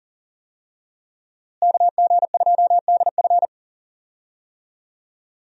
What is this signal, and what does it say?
Morse code, heard as a keyed tone, about 40 wpm, 700 Hz. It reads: KG2DF
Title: USA Call Signs - 5 Characters - Repeated 3x - 40wpm